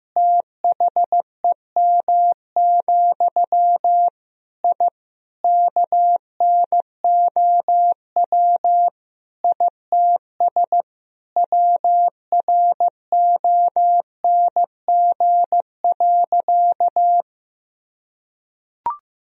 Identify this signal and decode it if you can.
Morse code, heard as a keyed tone, about 15 wpm, 700 Hz. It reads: THEM, I KNOW ITS WRONG.
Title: Sentences from Top 1000 Words - Review Part 1 - 15wpm